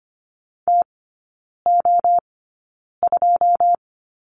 Morse code, heard as a keyed tone, about 25 wpm, spaced as for 9 wpm, 700 Hz. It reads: TO2
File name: course / 25:10wpm / 093 Lesson 16e - Letter c - Three Characters 25wpm